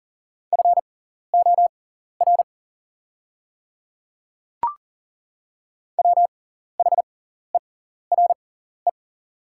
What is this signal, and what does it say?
Morse code, heard as a keyed tone, about 40 wpm, spaced as for 14 wpm, 700 Hz. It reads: FOR WHERE
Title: 6x Character Spacing - Top 200 Words - 40wpm